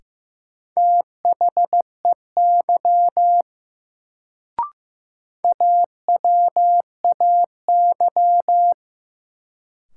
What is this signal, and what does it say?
Morse code, heard as a keyed tone, about 15 wpm, 700 Hz. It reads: THEY AWAY
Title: Sets of 2 Words - New Word Drill - 15wpm